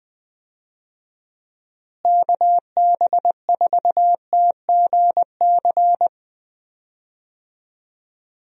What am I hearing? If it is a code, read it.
Morse code, heard as a keyed tone, about 21 wpm, 700 Hz. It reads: KB4TGC
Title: USA Call Signs - 6 Characters - Repeated 2x - 20wpm